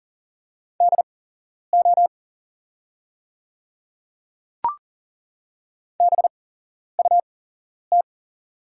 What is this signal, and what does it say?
Morse code, heard as a keyed tone, about 40 wpm, spaced as for 11 wpm, 700 Hz. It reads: DO BUT
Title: 8x Character Spacing - Top 100 Words - 40wpm